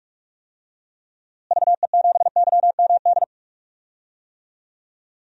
Morse code, heard as a keyed tone, about 46 wpm, 700 Hz. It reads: VE7YMD
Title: DXCC-100 Call Signs - 6 Characters - Repeated 2x - 45wpm